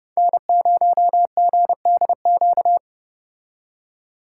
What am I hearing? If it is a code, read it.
Morse code, heard as a keyed tone, about 29 wpm, 700 Hz. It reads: N0GDQ